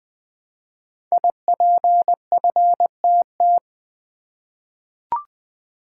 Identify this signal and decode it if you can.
Morse code, heard as a keyed tone, about 20 wpm, 700 Hz. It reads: IPFTT
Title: Code Groups - 5 Letters - 20wpm